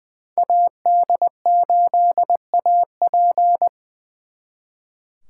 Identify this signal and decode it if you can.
Morse code, heard as a keyed tone, about 24 wpm, 700 Hz. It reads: AD8AP
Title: Speed Racing - USA 2x2 Call Signs 20wpm